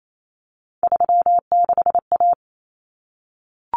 Morse code, heard as a keyed tone, about 28 wpm, 700 Hz. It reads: 36A
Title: Common Cut Numbers - 3-Digit 28wpm